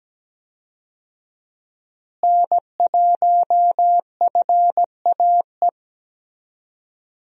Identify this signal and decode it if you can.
Morse code, heard as a keyed tone, about 17 wpm, 700 Hz. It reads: N1FAE
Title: USA Call Signs - 5 Characters - Repeated 3x - 17wpm